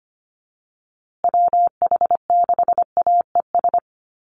Morse code, heard as a keyed tone, about 25 wpm, 700 Hz. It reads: WH6AES
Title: USA Call Signs Repeated 3x 25wpm